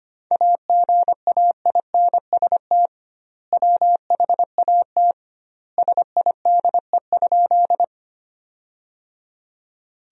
Morse code, heard as a keyed tone, about 25 wpm, 700 Hz. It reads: AGAINST WHAT SIDE?
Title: 2x Word-Spacing - Sentences from Top 400 Words - 25wpm